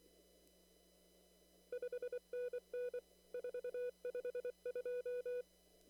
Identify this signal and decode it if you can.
Morse code, heard as a keyed tone, about 24 wpm, 490 Hz. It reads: 5NN 452
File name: CW / kt5j wpx cw-op-k5pi